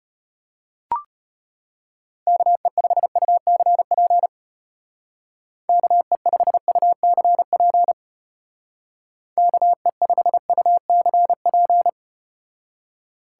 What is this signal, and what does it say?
Morse code, heard as a keyed tone, about 34 wpm, 700 Hz. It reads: KE5UCP KE5UCP KE5UCP